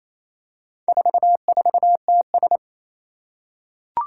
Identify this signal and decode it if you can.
Morse code, heard as a keyed tone, about 28 wpm, 700 Hz. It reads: 44TS
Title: Four Challenging Letter-Number Combinations 28wpm